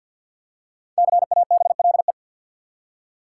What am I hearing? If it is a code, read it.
Morse code, heard as a keyed tone, about 49 wpm, 700 Hz. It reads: CABLE
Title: Five Letter Words - 50wpm